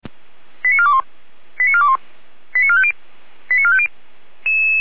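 Five-tone selective call (ZVEI), heard as quick runs of five tones, tones of about 70 ms. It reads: 89421 89421 89450 89450
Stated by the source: ZVEI five-tone selective call received with an SDR.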